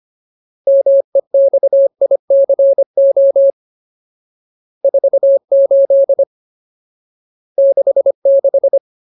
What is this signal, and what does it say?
Morse code, heard as a keyed tone, about 25 wpm, 550 Hz. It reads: MEXICO 48 66
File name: Ultimate Word List / 25WPM Ultimate